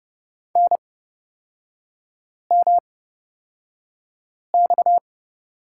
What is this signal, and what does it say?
Morse code, heard as a keyed tone, about 30 wpm, spaced as for 5 wpm, 700 Hz. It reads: NMX
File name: Lesson 37e - Letter x - Three Characters 30wpm